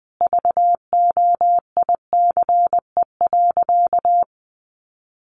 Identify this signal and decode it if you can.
Morse code, heard as a keyed tone, about 19 wpm, 700 Hz. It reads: VOICE.
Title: Sentences from Top 700 Words 20wpm